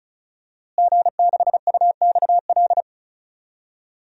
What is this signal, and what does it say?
Morse code, heard as a keyed tone, about 36 wpm, 700 Hz. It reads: G6UXL
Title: DXCC-100 Call Signs - 5 Characters - Repeated 3x - 35wpm